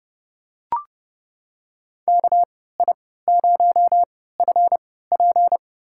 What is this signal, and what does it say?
Morse code, heard as a keyed tone, about 31 wpm, spaced as for 17 wpm, 700 Hz. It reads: KI0FP
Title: USA Call Signs - 5 Characters - 3x Character Spacing - 30wpm